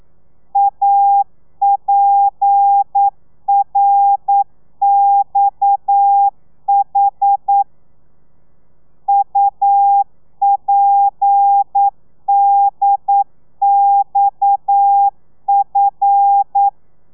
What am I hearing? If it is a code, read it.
Morse code, heard as a keyed tone, about 9 wpm, 800 Hz. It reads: APRXH UPDXF
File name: telegrafia / i2tfj / Lez7 40